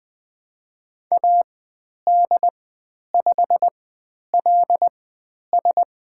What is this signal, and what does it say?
Morse code, heard as a keyed tone, about 20 wpm, spaced as for 10 wpm, 700 Hz. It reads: AD5LS
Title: 109 Lesson 18g - Letter m - Call Signs 20wpm